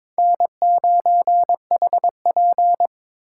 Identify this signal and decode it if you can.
Morse code, heard as a keyed tone, about 26 wpm, 700 Hz. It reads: N9HP
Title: Speed Racing - USA Call Signs 22wpm